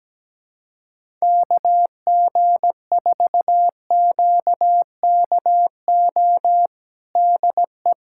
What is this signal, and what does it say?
Morse code, heard as a keyed tone, about 17 wpm, 700 Hz. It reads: KG4QKO DE
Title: USA Callsign DE Callsign 17wpm